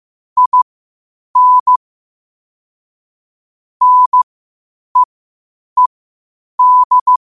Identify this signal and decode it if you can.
Morse code, heard as a keyed tone, about 15 wpm, spaced as for 8 wpm, 1000 Hz. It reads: IN NEED
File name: qotdfiles / 8wpm